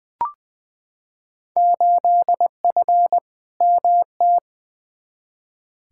Morse code, heard as a keyed tone, about 20 wpm, 700 Hz. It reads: 8F MT
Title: Field Day Exchanges Encore - 20wpm